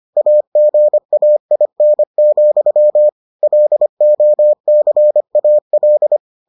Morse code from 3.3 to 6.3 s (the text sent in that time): LOCAL